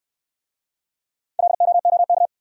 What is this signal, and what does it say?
Morse code, heard as a keyed tone, about 65 wpm, 700 Hz. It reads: LOOK